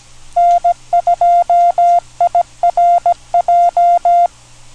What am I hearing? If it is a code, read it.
Morse code, heard as a keyed tone, about 17 wpm, 700 Hz. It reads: N2IRJ